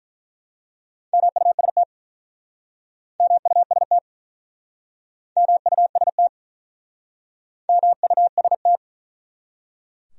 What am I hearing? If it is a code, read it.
Morse code, heard as a keyed tone, about 48 wpm, 700 Hz. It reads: MUST MUST MUST MUST